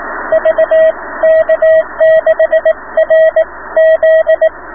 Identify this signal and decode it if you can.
Morse code, heard as a keyed tone, about 18 wpm, spaced as for 16 wpm, 690 Hz, with radio noise. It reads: VK6RZ